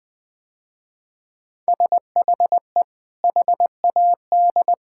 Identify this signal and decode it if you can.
Morse code, heard as a keyed tone, about 21 wpm, 700 Hz. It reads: SHE HAD
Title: Sentences with 2-Word Phrases - Spoken Beginning - 20wpm